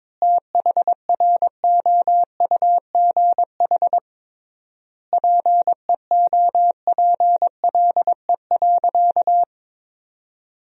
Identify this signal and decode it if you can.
Morse code, heard as a keyed tone, about 22 wpm, 700 Hz. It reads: THROUGH PEOPLE.
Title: 3x Word-Spacing - Sentences from Top 200 Words - 22wpm